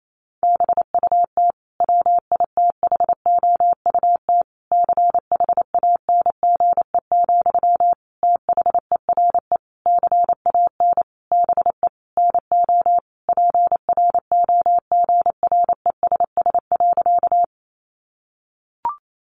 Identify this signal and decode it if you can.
Morse code, heard as a keyed tone, about 28 wpm, 700 Hz. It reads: BUT WITHOUT CHANGE, THERE CAN BE NO PROGRESS.
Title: Sentences from Top 1400 Words 28wpm